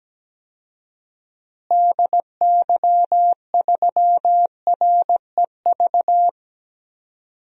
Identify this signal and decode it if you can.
Morse code, heard as a keyed tone, about 17 wpm, 700 Hz. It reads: DY3REV
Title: DXCC-100 Call Signs - 6 Characters - Repeated 3x - 17wpm